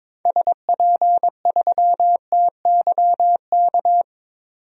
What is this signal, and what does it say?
Morse code, heard as a keyed tone, about 22 wpm, 700 Hz. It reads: SP3TYK
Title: DXCC-100 Call Signs - 6 Characters - 22wpm